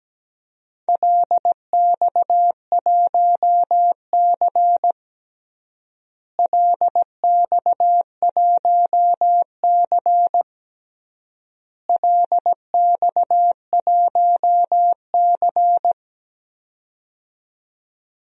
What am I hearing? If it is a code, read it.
Morse code, heard as a keyed tone, about 17 wpm, 700 Hz. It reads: LX1C LX1C LX1C
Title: DXCC-100 Call Signs - 4 Characters - Repeated 3x - 17wpm